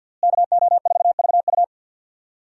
Morse code, heard as a keyed tone, about 48 wpm, 700 Hz. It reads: KO4VU
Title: USA Call Signs - 5 Characters - Repeated 2x - 50wpm